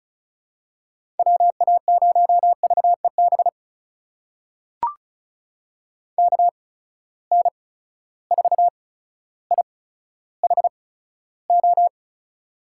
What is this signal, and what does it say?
Morse code, heard as a keyed tone, about 36 wpm, spaced as for 15 wpm, 700 Hz. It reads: WA0VEB KN4IHO
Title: USA Call Signs - 6 Characters - 8x Character Spacing - 35wpm